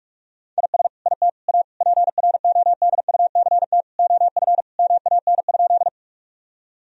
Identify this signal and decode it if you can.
Morse code, heard as a keyed tone, about 47 wpm, 700 Hz. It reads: IS IT A PRODUCT OF MAN?
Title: Sentences from Top 1000 Words - Review Part 2 - 45wpm